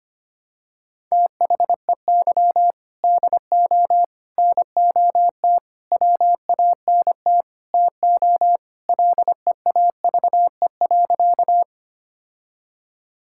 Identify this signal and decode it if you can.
Morse code, heard as a keyed tone, about 25 wpm, 700 Hz. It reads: THEY DO NOT WANT TO LEAVE.
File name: practice / 25wpm / Sentences with 3-Word Phrases - 25wpm